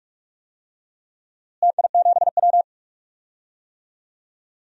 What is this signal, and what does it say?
Morse code, heard as a keyed tone, about 42 wpm, 700 Hz. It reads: TI7W
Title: DXCC-100 Call Signs - 4 Characters - Repeated 3x - 45wpm